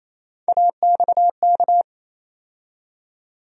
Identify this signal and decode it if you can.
Morse code, heard as a keyed tone, about 28 wpm, 700 Hz. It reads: AXK
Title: USA Call Sign Postfixes - 3 Letters 28wpm